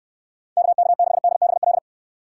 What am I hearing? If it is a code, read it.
Morse code, heard as a keyed tone, about 68 wpm, 700 Hz. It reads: KK6DBL